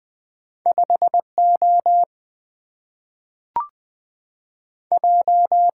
Morse code, heard as a keyed tone, about 20 wpm, 700 Hz. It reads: 5O J
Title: Two Letter-Number Combinations - 20wpm